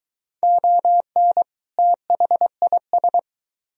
Morse code, heard as a keyed tone, about 23 wpm, 700 Hz. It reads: ON THIS